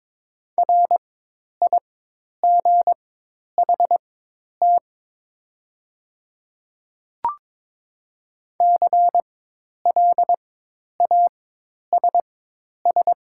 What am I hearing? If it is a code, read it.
Morse code, heard as a keyed tone, about 22 wpm, spaced as for 10 wpm, 700 Hz. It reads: RIGHT CLASS
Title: 4x Character Spacing - Top 500 Words - 22wpm